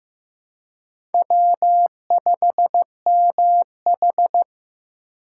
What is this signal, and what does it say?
Morse code, heard as a keyed tone, about 15 wpm, 700 Hz. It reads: W5MH
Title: USA Call Signs - 4 Characters - 2x Character Spacing - 15wpm